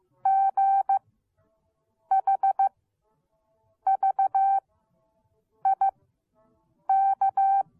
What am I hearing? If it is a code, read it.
Morse code, heard as a keyed tone, about 15 wpm, spaced as for 6 wpm, 780 Hz. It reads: GHVIK